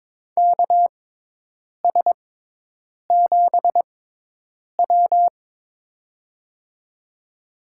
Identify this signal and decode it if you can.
Morse code, heard as a keyed tone, about 22 wpm, spaced as for 8 wpm, 700 Hz. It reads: KS7W
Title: USA Call Signs - 6x Character Spacing - 22wpm